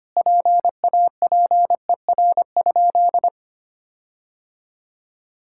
Morse code, heard as a keyed tone, about 25 wpm, 700 Hz. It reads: PAPER?
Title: Sentences from Top 1600 Words 25wpm